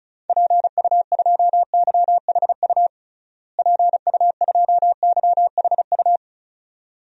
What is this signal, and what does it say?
Morse code, heard as a keyed tone, about 36 wpm, 700 Hz. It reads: PU2YHU PU2YHU
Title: DXCC-100 Call Signs - 6 Characters - Repeated 2x - 35wpm